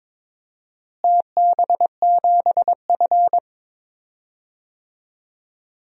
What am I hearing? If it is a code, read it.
Morse code, heard as a keyed tone, about 21 wpm, 700 Hz. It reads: TB7F
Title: DXCC-100 Call Signs - 4 Characters - Repeated 2x - 22wpm